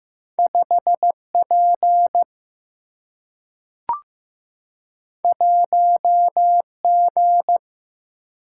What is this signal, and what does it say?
Morse code, heard as a keyed tone, about 15 wpm, 700 Hz. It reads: 5 P 1 G
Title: Single-Letter Single-Number 15wpm